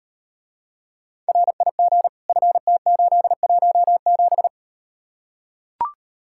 Morse code, heard as a keyed tone, about 38 wpm, 700 Hz. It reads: RIG FT817